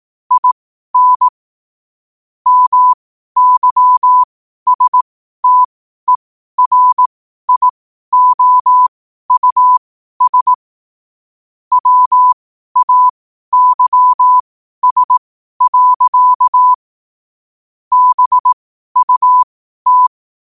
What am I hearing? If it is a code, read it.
Morse code, heard as a keyed tone, about 18 wpm, spaced as for 12 wpm, 1000 Hz. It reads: IN MYSTERIOUS WAYS. BUT
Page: 12 wpm morse code quotes